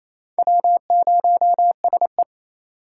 Morse code, heard as a keyed tone, about 28 wpm, 700 Hz. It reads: W0SE
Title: USA Call Signs - 4 Characters - 2x Character Spacing - 28wpm